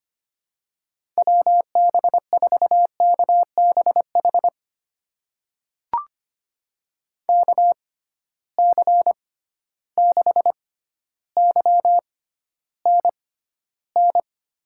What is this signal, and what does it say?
Morse code, heard as a keyed tone, about 25 wpm, spaced as for 13 wpm, 700 Hz. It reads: WB4KBH KC6YNN
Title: USA Call Signs - 6 Characters - 6x Character Spacing - 25wpm